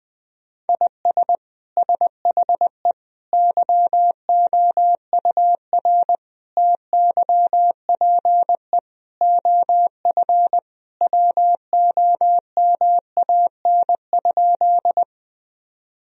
Morse code, heard as a keyed tone, about 20 wpm, 700 Hz. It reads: IS SHE YOUR TYPE OF WOMAN?
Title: Sentences from Top 1000 Words - Review Part 3 - 20wpm